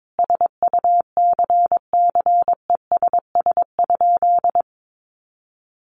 Morse code, heard as a keyed tone, about 22 wpm, 700 Hz. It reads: SUCCESS?
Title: Sentences from Top 900 Words 22wpm